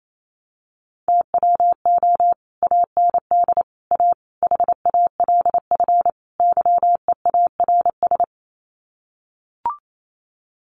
Morse code, heard as a keyed tone, about 28 wpm, 700 Hz. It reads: TWO AND A HALF YEARS
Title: Sets of 5 Words 28wpm